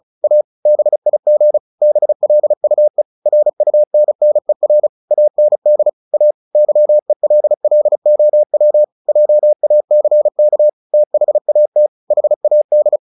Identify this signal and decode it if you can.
Morse code, heard as a keyed tone, about 34 wpm, 600 Hz. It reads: A BIG BLUE RUNNER AND A YELLOW JACK THAT HAD
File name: the old man and the sea / the old man and the sea 1 35wpm